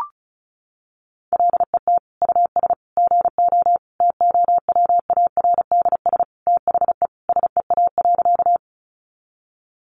Morse code, heard as a keyed tone, about 34 wpm, 700 Hz. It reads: LET US GO TOWARDS THE SEA.
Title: Sentences from Top 1100 Words 35wpm